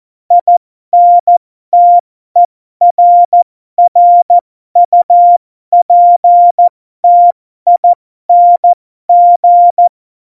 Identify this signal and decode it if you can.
Morse code, heard as a keyed tone, about 14 wpm, spaced as for 12 wpm, 700 Hz. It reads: INTERRUPTING
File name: Morsekurs / Witze